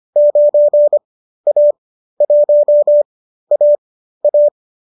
Morse code, heard as a keyed tone, about 25 wpm, spaced as for 13 wpm, 590 Hz. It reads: 9A1AA
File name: qso219 13